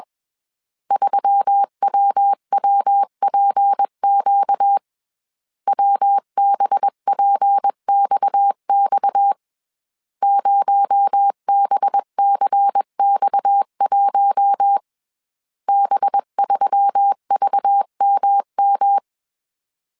Morse code, heard as a keyed tone, about 21 wpm, 800 Hz. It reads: 3WWPQ WBPXX 0BCX1 B3VMM
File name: CW / DET / 1 6 1500C S B 20wpm 2048